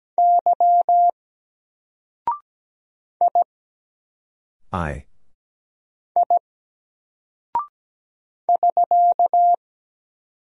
Morse code, heard as a keyed tone, about 17 wpm, 700 Hz. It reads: Y I I <SK>
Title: Single Character (Letter:Number:Punctuation:Prosign) 17wpm